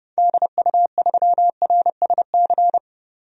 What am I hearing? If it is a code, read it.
Morse code, heard as a keyed tone, about 30 wpm, 700 Hz. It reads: DU3RSC
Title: DXCC-100 Call Signs - 6 Characters - 30wpm